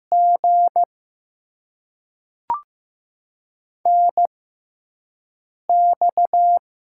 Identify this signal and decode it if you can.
Morse code, heard as a keyed tone, about 15 wpm, spaced as for 4 wpm, 700 Hz. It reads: GNX
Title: USA Call Signs - 4 Characters - 6x Character Spacing - 15wpm